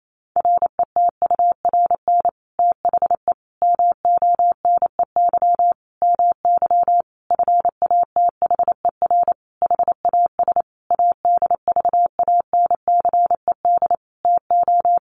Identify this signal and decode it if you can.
Morse code, heard as a keyed tone, about 28 wpm, 700 Hz. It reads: RETURN THE MONEY MY FATHER HAS ADVANCED TO
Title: Sentences from Top 1200 Words 28wpm